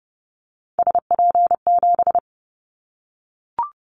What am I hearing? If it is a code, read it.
Morse code, heard as a keyed tone, about 30 wpm, 700 Hz. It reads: SP7
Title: Callsign Prefixes for Top 25 DXCC - Repeated 3x - 30wpm